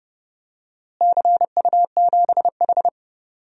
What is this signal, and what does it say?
Morse code, heard as a keyed tone, about 30 wpm, 700 Hz. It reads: CU7H
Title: DXCC-100 Call Signs - 4 Characters - Repeated 3x - 30wpm